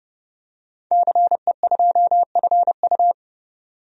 Morse code, heard as a keyed tone, about 30 wpm, 700 Hz. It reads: CE2FU
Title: DXCC-100 Call Signs - 5 Characters - 30wpm